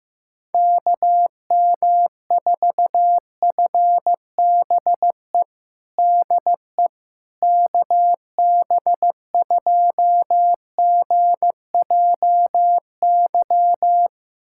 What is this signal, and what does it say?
Morse code, heard as a keyed tone, about 15 wpm, 700 Hz. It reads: KM4FBE DE KB2GJY
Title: USA Callsign DE Callsign - 6 Characters - 15wpm